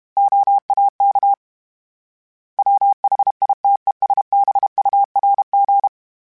Morse code, heard as a keyed tone, about 32 wpm, 800 Hz. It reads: OAK WHITESBURG
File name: fromSMs / FromSMs mp3s / Cities@30